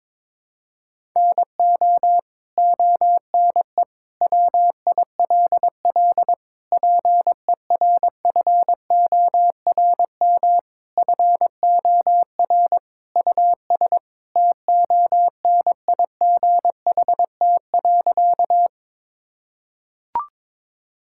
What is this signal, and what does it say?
Morse code, heard as a keyed tone, about 22 wpm, 700 Hz. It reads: NO ONE WILL PERFORM FOR US TONIGHT.
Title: Sentences from Top 1000 Words - Review Part 5 - 22wpm